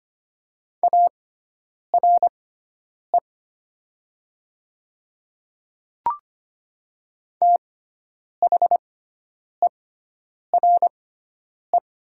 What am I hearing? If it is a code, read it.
Morse code, heard as a keyed tone, about 25 wpm, spaced as for 9 wpm, 700 Hz. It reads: ARE THERE
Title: 6x Character Spacing - Top 200 Words - 25wpm